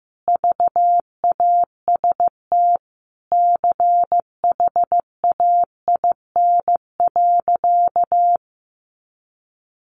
Morse code, heard as a keyed tone, about 15 wpm, 700 Hz. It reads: VAST CHAIN.
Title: Sentences from Top 1500 Words 15wpm